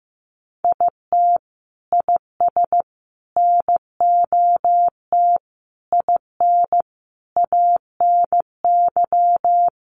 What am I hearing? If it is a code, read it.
Morse code, heard as a keyed tone, about 15 wpm, 700 Hz. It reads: IT IS NOT IN ANY
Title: Sentences from Top 1200 Words 15wpm